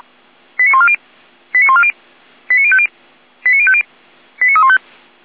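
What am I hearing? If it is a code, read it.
Five-tone selective call (ZVEI), heard as quick runs of five tones, tones of about 70 ms. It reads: 89140 89140 89050 89050 89315
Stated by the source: ZVEI five-tone selective call received by SDR.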